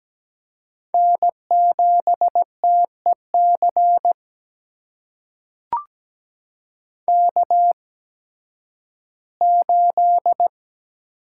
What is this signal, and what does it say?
Morse code, heard as a keyed tone, about 17 wpm, spaced as for 11 wpm, 700 Hz. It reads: N7TEC K8
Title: USA Call Signs - 5 Characters - 8x Character Spacing - 17wpm